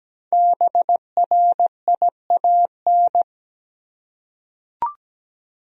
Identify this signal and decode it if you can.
Morse code, heard as a keyed tone, about 17 wpm, 700 Hz. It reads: BRIAN
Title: USA Common Names - Max 5 Letters - 17wpm